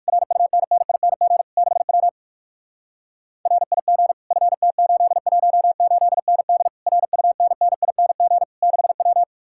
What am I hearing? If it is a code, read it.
Morse code, heard as a keyed tone, about 45 wpm, 700 Hz. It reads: RUNNING 6W RIG FT818ND RUNNING 6W
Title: Speed-Racing Common QSO Phrases - 45wpm